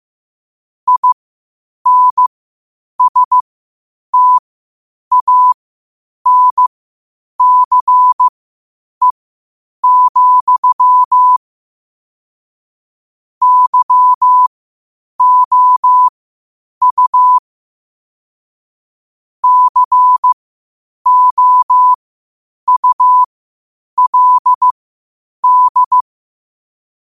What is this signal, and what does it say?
Morse code, heard as a keyed tone, about 15 wpm, spaced as for 8 wpm, 1000 Hz. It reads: INSTANCE, YOU COULD
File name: qotdfiles / 8wpm